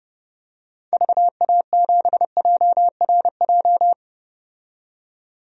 Morse code, heard as a keyed tone, about 30 wpm, 700 Hz. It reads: VA7JRJ
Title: DXCC-100 Call Signs - 6 Characters - 30wpm